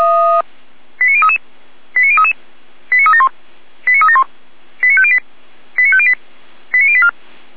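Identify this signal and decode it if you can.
Five-tone selective call (ZVEI), heard as quick runs of five tones, tones of about 70 ms. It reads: 89033 89033 89371 89371 89508 89508 89084
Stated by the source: ZVEI five-tone selective call received on an SDR.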